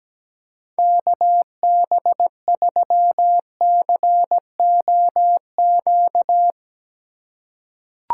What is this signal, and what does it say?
Morse code, heard as a keyed tone, about 17 wpm, 700 Hz. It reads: KB3COQ
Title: USA Call Signs - 6 Characters - 3x Character Spacing - 17wpm